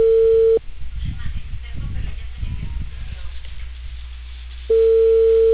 Ringback tone (China, 450 Hz).